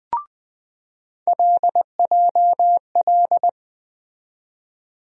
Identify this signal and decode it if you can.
Morse code, heard as a keyed tone, about 20 wpm, 700 Hz. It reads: LJL
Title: USA Call Sign Postfixes - 3 Letters 20wpm